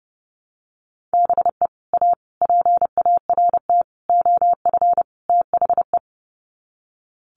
Morse code, heard as a keyed tone, about 30 wpm, 700 Hz. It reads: BE A PART OF THE
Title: Sets of 5 Words Encore 30wpm